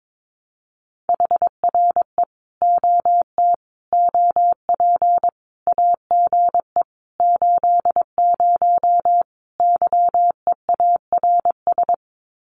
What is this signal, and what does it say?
Morse code, heard as a keyed tone, about 22 wpm, 700 Hz. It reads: HRE OT OP AGE 80 YEARS
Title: Common QSO Phrases 22wpm